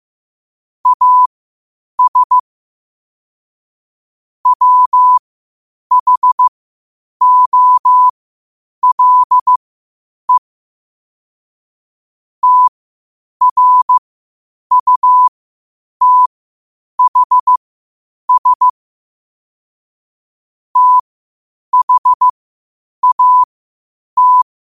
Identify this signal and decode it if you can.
Morse code, heard as a keyed tone, about 15 wpm, spaced as for 8 wpm, 1000 Hz. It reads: AS WHOLE TRUTHS THAT
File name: qotdfiles / 8wpm